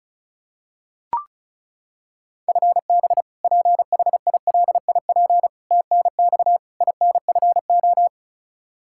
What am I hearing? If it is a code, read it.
Morse code, heard as a keyed tone, about 36 wpm, 700 Hz. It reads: FB PHILIP TNX INFO
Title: Practice QSO - Exchange 3 - More Info - 35wpm